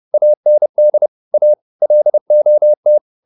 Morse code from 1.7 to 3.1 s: LOT

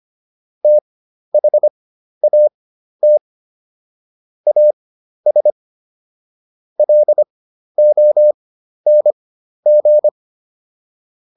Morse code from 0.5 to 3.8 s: THAT

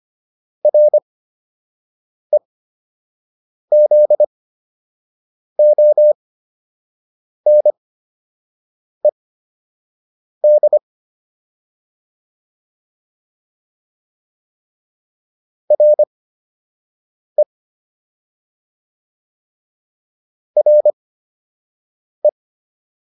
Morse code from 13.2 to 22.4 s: RE RE

